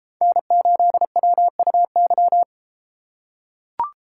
Morse code, heard as a keyed tone, about 33 wpm, 700 Hz. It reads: N8WUY